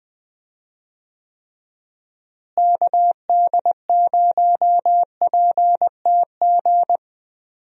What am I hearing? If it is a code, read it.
Morse code, heard as a keyed tone, about 20 wpm, 700 Hz. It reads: KD0PTG
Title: USA Call Signs - 6 Characters - Repeated 3x - 20wpm